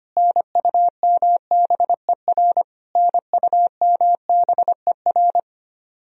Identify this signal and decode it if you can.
Morse code, heard as a keyed tone, about 25 wpm, 700 Hz. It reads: NUMBER NUMBER
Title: Top 100 Words Repeated 25wpm